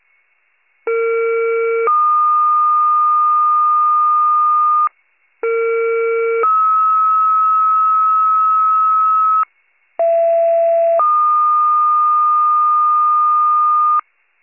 Two-tone page, heard as a one-second tone, then a three-second tone, about 460 then 1220 Hz, about 460 then 1290 Hz, about 670 then 1220 Hz. It is heard from 0.9 to 14.0 s.